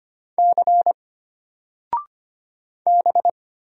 Morse code, heard as a keyed tone, about 25 wpm, 700 Hz. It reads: C B
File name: Single-Letter-Number-25wpm